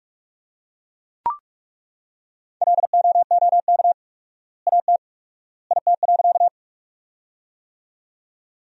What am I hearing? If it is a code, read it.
Morse code, heard as a keyed tone, about 41 wpm, 700 Hz. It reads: LOOK AT IT.
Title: 4x Word-Spacing - Sentences from Top 100 Words - 45wpm